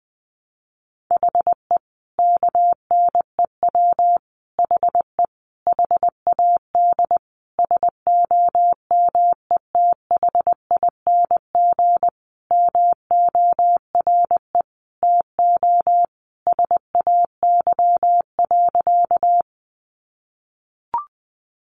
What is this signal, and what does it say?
Morse code, heard as a keyed tone, about 20 wpm, 700 Hz. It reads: HE KNEW HE HAD SOMETHING MORE TO SAY.
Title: Sentences from Top 600 Words 20wpm